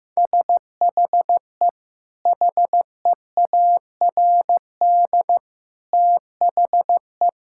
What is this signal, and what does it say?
Morse code, heard as a keyed tone, about 15 wpm, 700 Hz. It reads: SHE HEARD THE
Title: Sentences from Top 1600 Words 15wpm